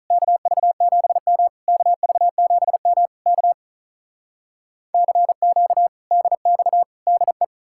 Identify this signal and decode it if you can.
Morse code, heard as a keyed tone, about 46 wpm, 700 Hz. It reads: KV7M KV7M K CQ DX DE